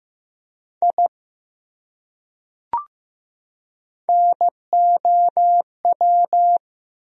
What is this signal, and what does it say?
Morse code, heard as a keyed tone, about 15 wpm, 700 Hz. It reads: I NOW
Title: Top 100 Words 15wpm